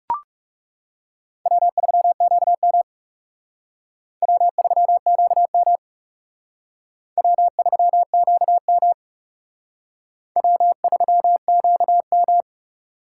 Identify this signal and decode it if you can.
Morse code, heard as a keyed tone, about 38 wpm, 700 Hz. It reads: W3QM W3QM W3QM W3QM